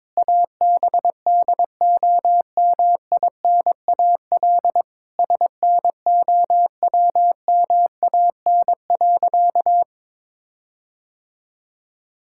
Morse code, heard as a keyed tone, about 21 wpm, 700 Hz. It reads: ABDOMINAL SNOWMAN.
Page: morse-code-ninja-practice-22wpm